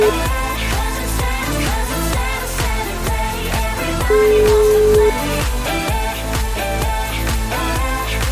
Ringback tone (Europe, 425 Hz), with noise.